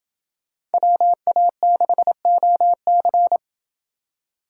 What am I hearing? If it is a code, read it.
Morse code, heard as a keyed tone, about 29 wpm, 700 Hz. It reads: WA6OC